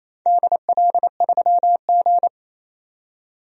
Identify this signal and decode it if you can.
Morse code, heard as a keyed tone, about 27 wpm, 690 Hz. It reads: DL3G
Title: DXCC-100 Call Signs - 4 Characters - 28wpm